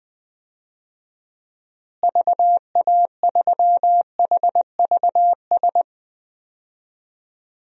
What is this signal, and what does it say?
Morse code, heard as a keyed tone, about 21 wpm, 700 Hz. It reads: VA3HVS
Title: DXCC-100 Call Signs - 6 Characters - Repeated 2x - 20wpm